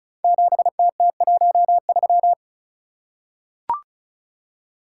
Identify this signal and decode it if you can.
Morse code, heard as a keyed tone, about 35 wpm, 700 Hz. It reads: 7TT13
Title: Code Groups - 5 Letter:Numbers - 35wpm